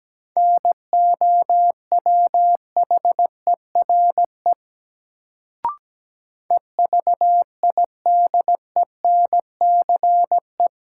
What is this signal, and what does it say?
Morse code, heard as a keyed tone, about 17 wpm, 700 Hz. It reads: NOWHERE EVIDENCE
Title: Sets of 5 Words Encore - New Word Drill - 17wpm